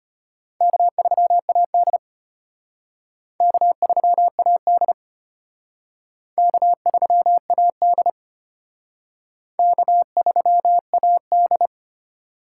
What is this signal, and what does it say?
Morse code, heard as a keyed tone, about 31 wpm, 700 Hz. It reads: K3AD K3AD K3AD K3AD
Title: Speed Racing - USA 1x2 Call Signs 25wpm